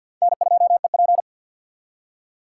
Morse code, heard as a keyed tone, about 62 wpm, 700 Hz. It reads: N2EP
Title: Speed Racing - USA 1x2 Call Signs 50wpm